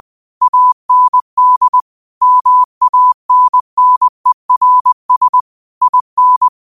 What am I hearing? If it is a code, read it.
Morse code, heard as a keyed tone, about 20 wpm, 1000 Hz. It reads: AND MANNERS IN